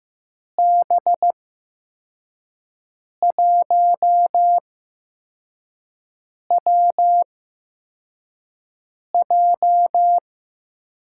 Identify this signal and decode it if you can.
Morse code, heard as a keyed tone, about 15 wpm, spaced as for 4 wpm, 700 Hz. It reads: B1WJ
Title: USA Call Signs - 6 Characters - 8x Character Spacing - 15wpm